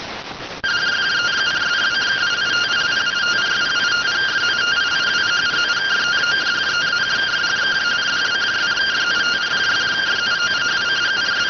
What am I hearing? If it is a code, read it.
Two-tone teleprinter signal, about 170 Hz shift.